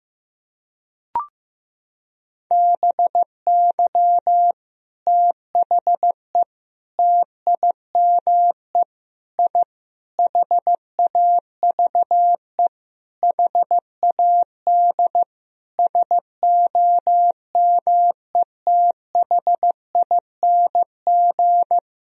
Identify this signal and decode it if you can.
Morse code, heard as a keyed tone, about 15 wpm, 700 Hz. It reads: BY THE TIME I HAVE HAD SOMETHING
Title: Sentences from Top 1000 Words - Review Part 1 - 15wpm